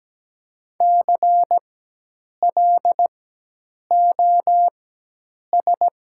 Morse code, heard as a keyed tone, about 17 wpm, spaced as for 8 wpm, 700 Hz. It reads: CLOS
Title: 4x Character Spacing - Top 500 Words - 17wpm